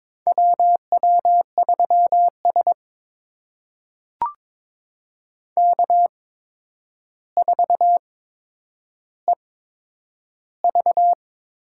Morse code, heard as a keyed tone, about 22 wpm, spaced as for 9 wpm, 700 Hz. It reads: WW3S K4EV
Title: USA Call Signs - 4 Characters - 8x Character Spacing - 22wpm